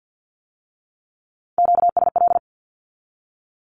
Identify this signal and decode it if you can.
Morse code, heard as a keyed tone, about 50 wpm, 700 Hz. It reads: QSL
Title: Q-Codes with Spoken Meaning - 50wpm